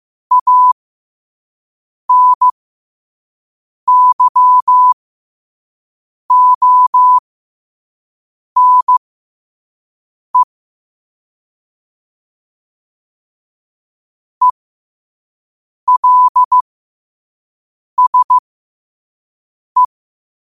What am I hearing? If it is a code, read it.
Morse code, heard as a keyed tone, about 15 wpm, spaced as for 5 wpm, 1000 Hz. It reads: ANYONE ELSE
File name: qotdfiles / 5wpm